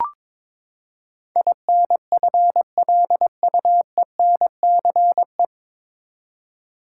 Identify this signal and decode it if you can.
Morse code, heard as a keyed tone, about 22 wpm, 700 Hz. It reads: INFLUENCE
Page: Morse Code Ninja practice 22wpm